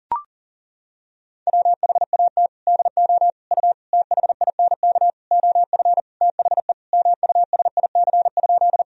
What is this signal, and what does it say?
Morse code, heard as a keyed tone, about 40 wpm, 700 Hz. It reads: WHAT DO U THINK OF THE MUSIC?
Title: Sentences from Top 300 Words and Abbreviations 40wpm